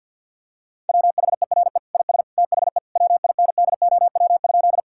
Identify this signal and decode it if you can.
Morse code, heard as a keyed tone, about 51 wpm, 700 Hz. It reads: WHERE IS THE WINDOW?
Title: Sentences from Top 1600 Words 50wpm